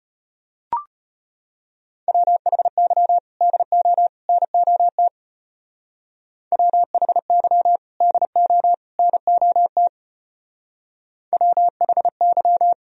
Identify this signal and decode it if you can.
Morse code, heard as a keyed tone, about 32 wpm, 700 Hz. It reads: WHY DO NOT WHY DO NOT WHY